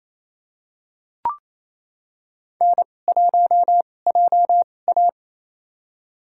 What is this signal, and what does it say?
Morse code, heard as a keyed tone, about 28 wpm, spaced as for 20 wpm, 700 Hz. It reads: N1JA
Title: USA Call Signs - 4 Characters - 2x Character Spacing - 28wpm